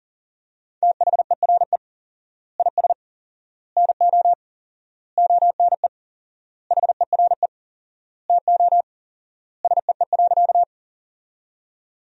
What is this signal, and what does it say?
Morse code, heard as a keyed tone, about 40 wpm, 700 Hz. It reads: THERE IS NO ONE HERE TO SEE.
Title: 4x Word-Spacing - Sentences from Top 200 Words - 40wpm